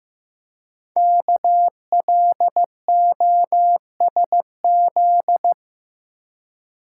Morse code, heard as a keyed tone, about 15 wpm, 700 Hz. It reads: KLOSZ
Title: Code Groups - 5 Letters - 15wpm